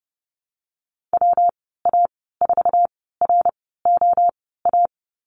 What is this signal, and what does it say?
Morse code, heard as a keyed tone, about 30 wpm, spaced as for 17 wpm, 700 Hz. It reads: WA4ROA
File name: Lesson 19g - Letter w - Call Signs 30wpm